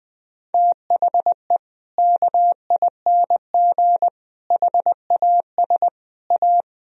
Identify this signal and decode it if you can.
Morse code, heard as a keyed tone, about 20 wpm, 700 Hz. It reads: THE KING HAS A
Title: Sentences from Top 1000 Words - Review Part 4 - 20wpm